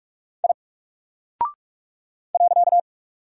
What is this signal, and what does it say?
Morse code, heard as a keyed tone, about 47 wpm, 700 Hz. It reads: I .
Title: Single Character (Letter:Number:Punctuation:Prosign) Alt 45wpm